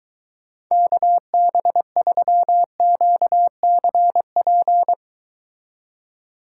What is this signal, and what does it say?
Morse code, heard as a keyed tone, about 23 wpm, 700 Hz. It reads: KB3QCP